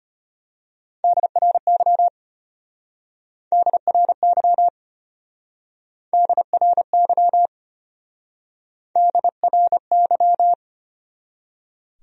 Morse code, heard as a keyed tone, about 32 wpm, 700 Hz. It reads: DRY DRY DRY DRY